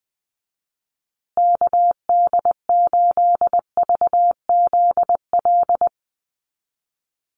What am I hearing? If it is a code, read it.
Morse code, heard as a keyed tone, about 20 wpm, 700 Hz. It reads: KD8VZL
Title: USA Call Signs Repeated 3x 20wpm